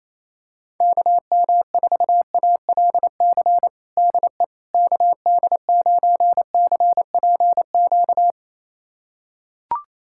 Morse code, heard as a keyed tone, about 28 wpm, 700 Hz. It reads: KM4ALC DE KD9CPQ
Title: USA Callsign DE Callsign 28wpm